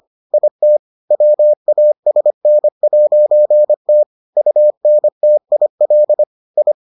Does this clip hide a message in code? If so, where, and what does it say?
Yes: Morse code from 0.3 to 6.7 s, heard as a keyed tone, about 25 wpm, 600 Hz. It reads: IT WASN'T UNTIL I